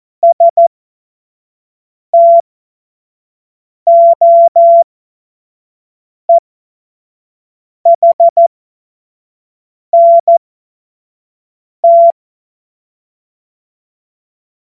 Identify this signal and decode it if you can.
Morse code, heard as a keyed tone, about 14 wpm, spaced as for 5 wpm, 680 Hz. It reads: STOEHNT
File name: Morsekurs / Witze / Witz 15. Juni 2021, 25 F70